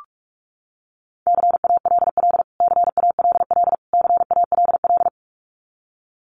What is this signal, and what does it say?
Morse code, heard as a keyed tone, about 44 wpm, 700 Hz. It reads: CALL CALL CALL
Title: Top 500 Words Repeated 45wpm